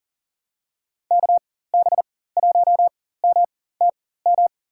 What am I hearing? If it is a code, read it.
Morse code, heard as a keyed tone, about 40 wpm, spaced as for 19 wpm, 700 Hz. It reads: KB1MTM
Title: USA Call Signs - 4x Character Spacing - 40wpm